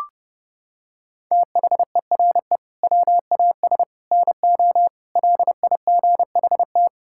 Morse code, heard as a keyed tone, about 30 wpm, 700 Hz. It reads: THERE WAS NO LIGHT
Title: Sentences from Top 1000 Words - Review Part 5 - 30wpm